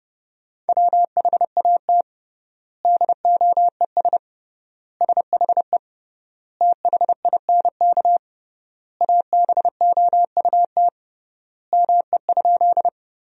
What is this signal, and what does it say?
Morse code, heard as a keyed tone, about 30 wpm, 700 Hz. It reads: WHAT DOES SHE THINK ABOUT ME?
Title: 3x Word-Spacing - Sentences from Top 200 Words - 30wpm